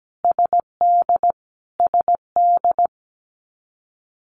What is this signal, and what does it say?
Morse code, heard as a keyed tone, about 17 wpm, 700 Hz. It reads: SD SD
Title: US States Repeated 3x 17wpm